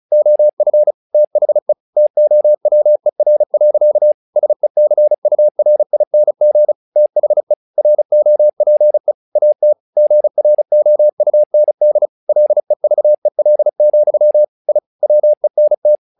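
Morse code, heard as a keyed tone, about 35 wpm, 600 Hz. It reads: OF THE TOWER. SECURING THE ROPE AT GROUND LEVEL, I WENT
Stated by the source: Morse practice text "Tower Trouble" sent at 35 WPM, from the SKCC - Straight Key Century Club learning center audio files.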